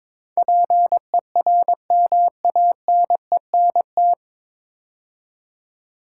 Morse code, heard as a keyed tone, about 23 wpm, 700 Hz. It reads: PERMANENT
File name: 9 Letter Words - First 3 Letters Given - 22wpm